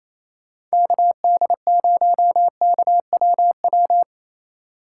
Morse code, heard as a keyed tone, about 34 wpm, 700 Hz. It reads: KD0KWW
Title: Speed Racing - USA 2x3 Call Signs 28wpm